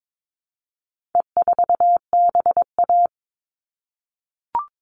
Morse code, heard as a keyed tone, about 22 wpm, 700 Hz. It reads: E4BA
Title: All Cut Numbers - 4-Digit 22wpm